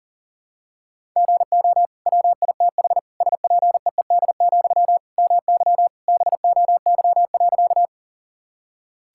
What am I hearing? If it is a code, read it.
Morse code, heard as a keyed tone, about 40 wpm, 700 Hz. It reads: GO WITH SPEED, MY BOY.
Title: Sentences from Top 1000 Words - Review Part 5 - 40wpm